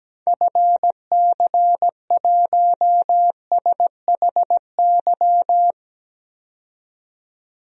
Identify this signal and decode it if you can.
Morse code, heard as a keyed tone, about 17 wpm, 700 Hz. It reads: FC1SHY
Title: DXCC-100 Call Signs - 6 Characters - 17wpm